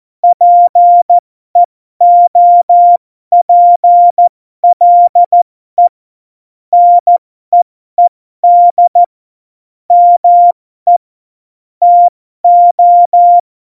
Morse code, heard as a keyed tone, about 14 wpm, spaced as for 12 wpm, 700 Hz. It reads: PEOPLE NEED ME TO
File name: Morsekurs / Witze